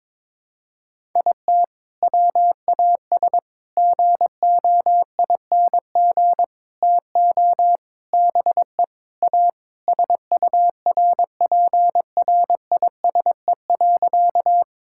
Morse code, heard as a keyed tone, about 22 wpm, 700 Hz. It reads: IT WAS GOING TO BE A SURPRISE.
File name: practice / 22wpm / Sentences with 4-Word Phrases - 22wpm